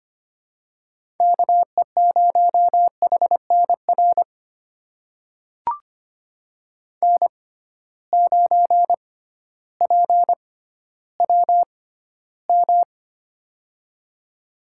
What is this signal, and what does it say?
Morse code, heard as a keyed tone, about 25 wpm, spaced as for 13 wpm, 700 Hz. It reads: KE0HNR N9PWM
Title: USA Call Signs - 6x Character Spacing - 25wpm